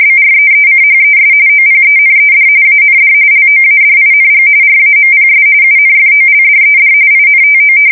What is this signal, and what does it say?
RTTY, about 45.45 baud. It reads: BREECHES OPENED AND LOADING RAMPS UNFOLDED INTO
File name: fsk170 / battleshipnj